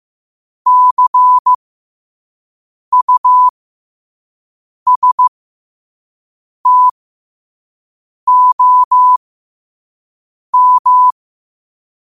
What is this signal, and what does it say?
Morse code, heard as a keyed tone, about 15 wpm, spaced as for 5 wpm, 1000 Hz. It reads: CUSTOM
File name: qotdfiles / 5wpm